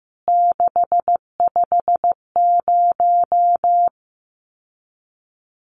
Morse code, heard as a keyed tone, about 15 wpm, 700 Hz. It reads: 650
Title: CWT Exchange 15wpm